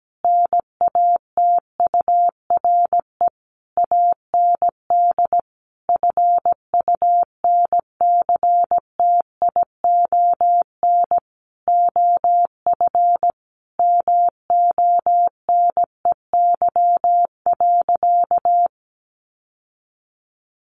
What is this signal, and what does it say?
Morse code, heard as a keyed tone, about 17 wpm, 700 Hz. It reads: NATURE AND FUNCTION OF MONEY.
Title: Sentences from Top 1100 Words 17wpm